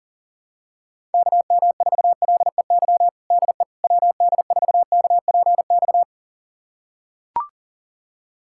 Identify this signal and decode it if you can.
Morse code, heard as a keyed tone, about 40 wpm, 700 Hz. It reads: KM4LEY DE WD4KPX
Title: USA Callsign DE Callsign 40wpm